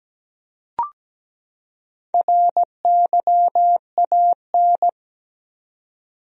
Morse code, heard as a keyed tone, about 17 wpm, 700 Hz. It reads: RYAN
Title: USA Common Names - 17wpm